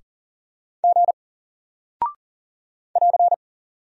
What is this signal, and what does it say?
Morse code, heard as a keyed tone, about 40 wpm, 700 Hz. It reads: G <AR>
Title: Single Character (Letter:Number:Punctuation:Prosign) Alt 40wpm